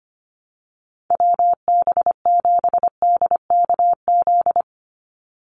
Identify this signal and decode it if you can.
Morse code, heard as a keyed tone, about 25 wpm, 700 Hz. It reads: WB7DKZ
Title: Call Signs - 25wpm